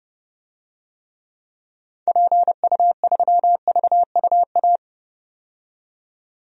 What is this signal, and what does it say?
Morse code, heard as a keyed tone, about 32 wpm, 700 Hz. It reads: PU3VUA
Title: DXCC-100 Call Signs - 6 Characters - 30wpm